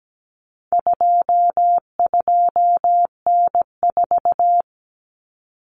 Morse code, heard as a keyed tone, about 17 wpm, 700 Hz. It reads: 22N4
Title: Common Cut Numbers - 4-Digit 17wpm